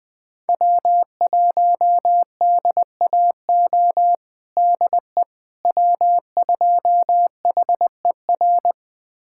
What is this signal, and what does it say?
Morse code, heard as a keyed tone, about 21 wpm, 700 Hz. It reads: W1DAO DE W2HER
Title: USA Callsign DE Callsign - 5 Characters - 20wpm